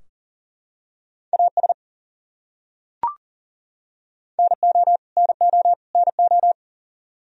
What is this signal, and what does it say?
Morse code, heard as a keyed tone, about 40 wpm, 700 Hz. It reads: AS NO NO NO
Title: Top 100 Words Repeated 40wpm